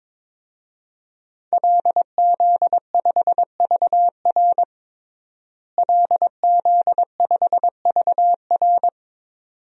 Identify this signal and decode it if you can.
Morse code, heard as a keyed tone, about 22 wpm, 700 Hz. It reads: LZ5VR LZ5VR
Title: DXCC-100 Call Signs - 5 Characters - Repeated 2x - 22wpm